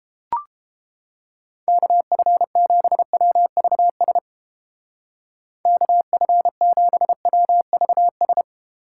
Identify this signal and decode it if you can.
Morse code, heard as a keyed tone, about 31 wpm, 700 Hz. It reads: KF7WVS KF7WVS